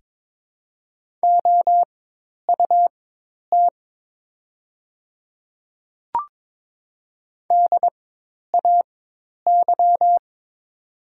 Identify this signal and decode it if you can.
Morse code, heard as a keyed tone, about 22 wpm, spaced as for 10 wpm, 700 Hz. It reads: OUT DAY
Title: 4x Character Spacing - Top 100 Words - 22wpm